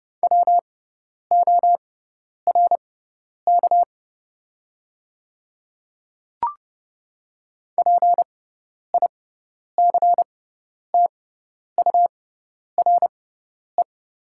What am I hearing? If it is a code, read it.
Morse code, heard as a keyed tone, about 30 wpm, spaced as for 10 wpm, 690 Hz. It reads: WORK PICTURE
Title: 6x Character Spacing - Top 200 Words - 30wpm